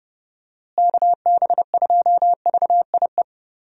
Morse code, heard as a keyed tone, about 32 wpm, 700 Hz. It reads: KB2VIE
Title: USA Call Signs - 6 Characters - Repeated 2x - 30wpm